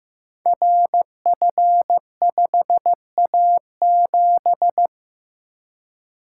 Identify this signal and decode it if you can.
Morse code, heard as a keyed tone, about 15 wpm, 700 Hz. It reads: RF5A7
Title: Code Groups - 5 Letter:Numbers - 15wpm